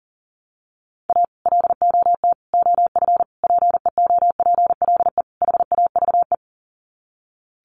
Morse code, heard as a keyed tone, about 40 wpm, 700 Hz. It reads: A LOT OF PEOPLE HAVE
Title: Sets of 5 Words 40wpm